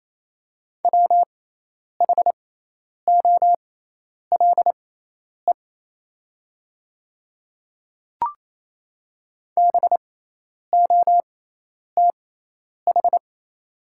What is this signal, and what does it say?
Morse code, heard as a keyed tone, about 28 wpm, spaced as for 10 wpm, 700 Hz. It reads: WHOLE BOTH
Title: 6x Character Spacing - Top 500 Words - 28wpm